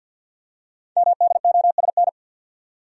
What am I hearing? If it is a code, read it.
Morse code, heard as a keyed tone, about 50 wpm, 700 Hz. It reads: MDOSN
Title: Code Groups - 5 Letter:Numbers - 50wpm